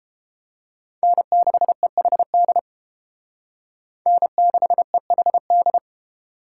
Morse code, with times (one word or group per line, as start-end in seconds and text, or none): 1.0-2.6 N6EHD
4.1-5.8 N6EHD